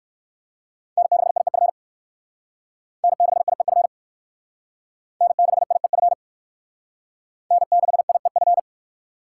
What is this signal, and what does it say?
Morse code, heard as a keyed tone, about 61 wpm, 700 Hz. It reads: N6IEF N6IEF N6IEF N6IEF